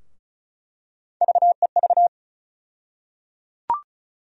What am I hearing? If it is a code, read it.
Morse code, heard as a keyed tone, about 36 wpm, 700 Hz. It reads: VEV
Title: USA Call Sign Postfixes - 3 Letters 35wpm